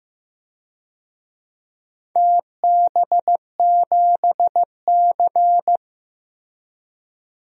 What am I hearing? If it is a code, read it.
Morse code, heard as a keyed tone, about 15 wpm, 700 Hz. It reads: TB7C
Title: DXCC-100 Call Signs - 4 Characters - Repeated 3x - 15wpm